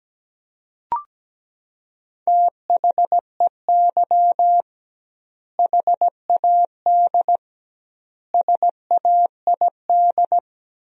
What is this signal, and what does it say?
Morse code, heard as a keyed tone, about 17 wpm, 700 Hz. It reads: THEY HAD SAID
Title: 2x Word-Spacing - Sentences from Top 400 Words - 17wpm